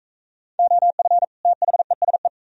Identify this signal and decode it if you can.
Morse code, heard as a keyed tone, about 42 wpm, 700 Hz. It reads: OF THESE